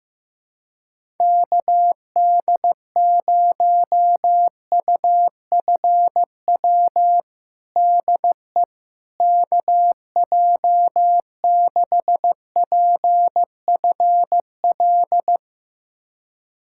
Morse code, heard as a keyed tone, about 15 wpm, 700 Hz. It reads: KD0UFW DE KJ6PFL
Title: USA Callsign DE Callsign - 6 Characters - 15wpm